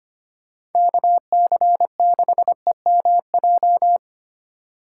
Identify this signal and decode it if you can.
Morse code, heard as a keyed tone, about 31 wpm, 700 Hz. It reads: KC6EMJ
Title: Speed Racing - USA 2x3 Call Signs 25wpm